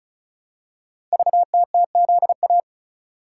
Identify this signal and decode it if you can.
Morse code, heard as a keyed tone, about 36 wpm, 700 Hz. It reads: VTTZA
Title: Code Groups - 5 Letters - 35wpm